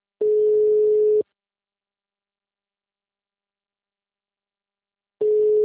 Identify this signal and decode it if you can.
Ringback tone (Europe, 425 Hz).